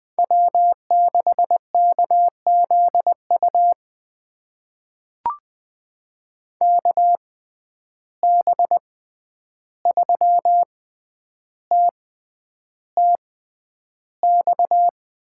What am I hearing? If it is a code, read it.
Morse code, heard as a keyed tone, about 20 wpm, spaced as for 10 wpm, 700 Hz. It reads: W6KZU KB3TTX
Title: USA Call Signs - 6x Character Spacing - 20wpm